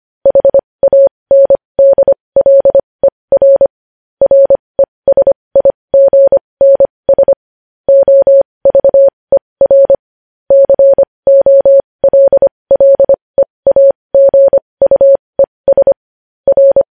Morse code, heard as a keyed tone, about 25 wpm, spaced as for 20 wpm, 550 Hz. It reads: HANDLER RESIGNS OVER COLLEAGUES R
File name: podcast / News headlines-1x-25x20